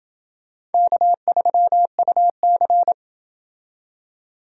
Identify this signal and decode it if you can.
Morse code, heard as a keyed tone, about 28 wpm, 700 Hz. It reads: K3UC